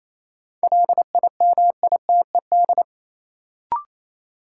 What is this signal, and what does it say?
Morse code, heard as a keyed tone, about 27 wpm, 700 Hz. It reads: LIMITED
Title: Sentences from Top 900 Words - New Word Drill - 28wpm